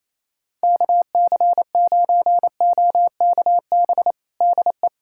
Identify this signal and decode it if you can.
Morse code, heard as a keyed tone, about 28 wpm, 700 Hz. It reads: KC9OKB DE
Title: USA Callsign DE Callsign - 6 Characters - 28wpm